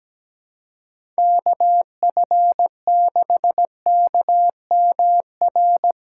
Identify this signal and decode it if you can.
Morse code, heard as a keyed tone, about 17 wpm, 700 Hz. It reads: KF6KMR
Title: USA Call Signs - 6 Characters - 4x Character Spacing - 17wpm